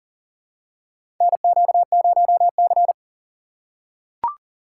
Morse code, heard as a keyed tone, about 40 wpm, 700 Hz. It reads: NQ0C
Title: USA Call Signs - 4 Characters - 4x Character Spacing - 40wpm